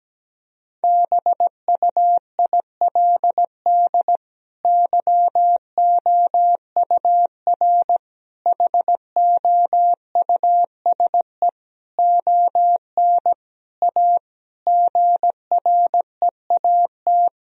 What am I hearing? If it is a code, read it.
Morse code, heard as a keyed tone, about 17 wpm, 700 Hz. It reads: BUILD YOUR HOUSE ON A GREAT
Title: Sentences from Top 1000 Words - Review Part 2 - 17wpm